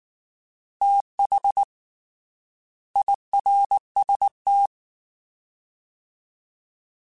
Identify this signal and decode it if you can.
Morse code, heard as a keyed tone, about 19 wpm, 800 Hz. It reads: TH IRST
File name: Fast TH @18